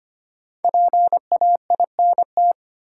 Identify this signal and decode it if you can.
Morse code, heard as a keyed tone, about 25 wpm, 700 Hz. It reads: PAINT
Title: Sentences from Top 1000 Words - Review Part 6 - New Word Drill - 25wpm